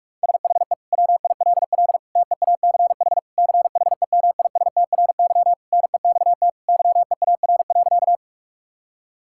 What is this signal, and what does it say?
Morse code, heard as a keyed tone, about 46 wpm, 700 Hz. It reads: SHE WILL TEACH CHEMISTRY NEXT YEAR.
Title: Sentences from Top 1000 Words - Review Part 6 - 45wpm